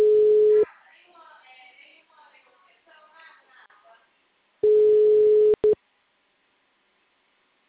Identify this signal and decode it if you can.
Ringback tone (Europe, 425 Hz).